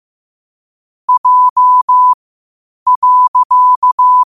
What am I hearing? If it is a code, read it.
Morse code, heard as a keyed tone, about 15 wpm, spaced as for 8 wpm, 1000 Hz. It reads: J.
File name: qotdfiles / 8wpm